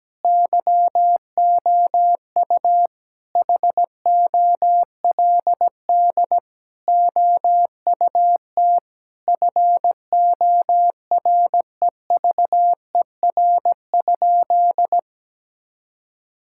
Morse code, heard as a keyed tone, about 17 wpm, 700 Hz. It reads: YOU HOLD OUT FOREVER?
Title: Sentences from Top 1600 Words 17wpm